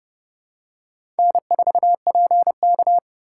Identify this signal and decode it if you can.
Morse code, heard as a keyed tone, about 28 wpm, 700 Hz. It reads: N4PK